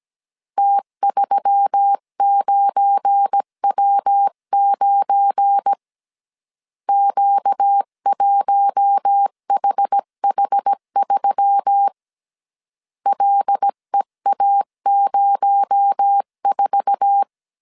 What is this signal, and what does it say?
Morse code, heard as a keyed tone, about 17 wpm, 800 Hz. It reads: T39W9 Q1HH3 LEA04
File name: CW / DET / 1 9 1500C S B 16wpm 2856